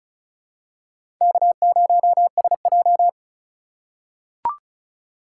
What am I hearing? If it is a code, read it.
Morse code, heard as a keyed tone, about 34 wpm, 700 Hz. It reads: K0SJ
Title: USA Call Signs - 4 Characters - 3x Character Spacing - 35wpm